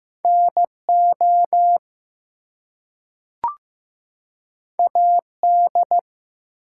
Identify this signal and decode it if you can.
Morse code, heard as a keyed tone, about 15 wpm, 700 Hz. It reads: NO AD
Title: Two Letter Words - 15wpm